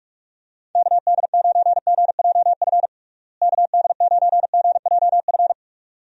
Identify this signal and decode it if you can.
Morse code, heard as a keyed tone, about 48 wpm, 700 Hz. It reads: KD9GJF KD9GJF
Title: USA Call Signs - 6 Characters - Repeated 2x - 45wpm